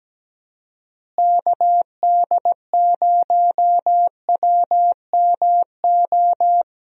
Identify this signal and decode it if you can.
Morse code, heard as a keyed tone, about 17 wpm, 700 Hz. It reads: KD0WMO
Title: USA Call Signs - 6 Characters - Repeated 3x - 17wpm